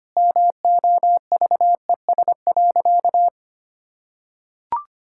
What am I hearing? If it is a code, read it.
Morse code, heard as a keyed tone, about 25 wpm, 700 Hz. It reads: MOVES.
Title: Pangram Challenge v2 - 25wpm